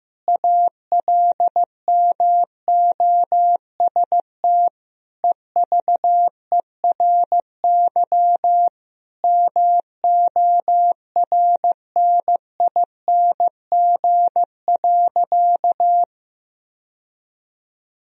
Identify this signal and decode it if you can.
Morse code, heard as a keyed tone, about 15 wpm, 700 Hz. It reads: ALMOST EVERY MORNING.
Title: Sentences from Top 1000 Words - Review Part 4 - 15wpm